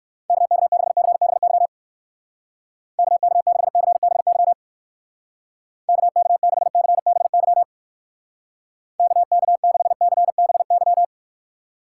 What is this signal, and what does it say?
Morse code, heard as a keyed tone, about 56 wpm, 700 Hz. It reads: KK6CBY KK6CBY KK6CBY KK6CBY
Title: Speed Racing - USA 2x3 Call Signs 45wpm